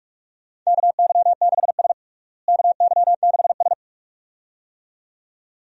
Morse code, heard as a keyed tone, about 46 wpm, 700 Hz. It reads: KY6S KY6S
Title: USA Call Signs - 4 Characters - Repeated 2x - 45wpm